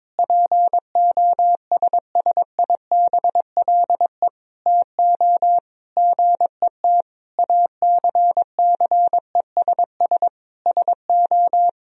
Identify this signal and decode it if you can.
Morse code, heard as a keyed tone, about 22 wpm, 700 Hz. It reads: POSSIBLE TO GET ACCESS SO
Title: Sentences from Top 1000 Words - Review Part 4 - 22wpm